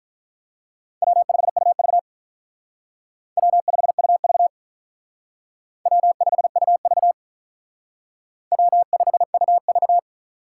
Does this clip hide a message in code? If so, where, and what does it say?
Yes: Morse code from 1.0 to 10.0 s, heard as a keyed tone, about 45 wpm, 700 Hz. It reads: W5UV W5UV W5UV W5UV